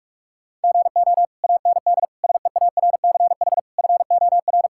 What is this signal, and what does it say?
Morse code, heard as a keyed tone, about 42 wpm, 690 Hz. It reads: GO AND SEARCH FOR
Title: Sentences from Top 1000 Words - Review Part 5 - 45wpm